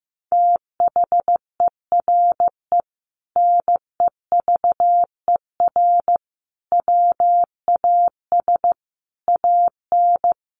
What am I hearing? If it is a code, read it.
Morse code, heard as a keyed tone, about 15 wpm, 700 Hz. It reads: THERE NEVER WAS AN
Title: Sentences from Top 1500 Words 15wpm